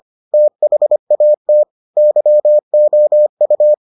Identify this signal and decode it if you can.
Morse code, heard as a keyed tone, about 25 wpm, 600 Hz. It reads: THAT YOU